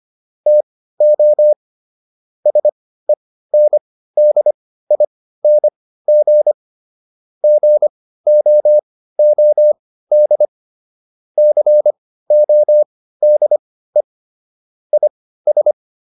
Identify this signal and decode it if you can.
Morse code, heard as a keyed tone, about 25 wpm, spaced as for 15 wpm, 600 Hz. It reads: TO SENDING GOOD CODE IS